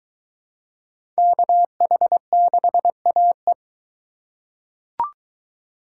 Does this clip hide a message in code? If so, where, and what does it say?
Yes: Morse code from 1.2 to 3.5 s, heard as a keyed tone, about 21 wpm, 700 Hz. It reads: KH6AE